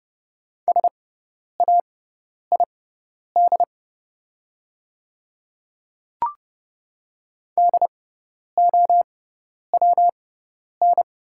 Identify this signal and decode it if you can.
Morse code, heard as a keyed tone, about 31 wpm, spaced as for 10 wpm, 700 Hz. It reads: SAID DOWN
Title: 6x Character Spacing - Top 100 Words - 30wpm